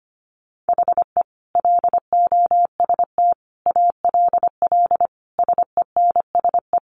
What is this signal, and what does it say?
Morse code, heard as a keyed tone, about 25 wpm, 700 Hz. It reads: HE LOST ALL SENSE
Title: Sentences from Top 1400 Words 25wpm